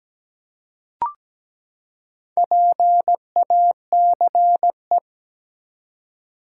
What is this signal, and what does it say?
Morse code, heard as a keyed tone, about 17 wpm, 700 Hz. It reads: PACE
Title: Four Letter Words - 17wpm